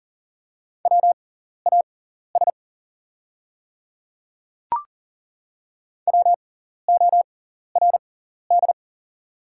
Morse code, heard as a keyed tone, about 40 wpm, spaced as for 14 wpm, 690 Hz. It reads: WAS WORD
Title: 6x Character Spacing - Top 100 Words - 40wpm